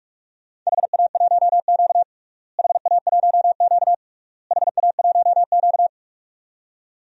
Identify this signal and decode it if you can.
Morse code, heard as a keyed tone, about 47 wpm, 700 Hz. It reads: HA1Q HA1Q HA1Q
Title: DXCC-100 Call Signs - 4 Characters - Repeated 3x - 45wpm